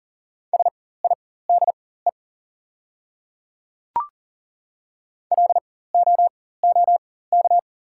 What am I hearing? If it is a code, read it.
Morse code, heard as a keyed tone, about 40 wpm, spaced as for 19 wpm, 690 Hz. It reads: SIDE LOOK
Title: 4x Character Spacing - Top 100 Words - 40wpm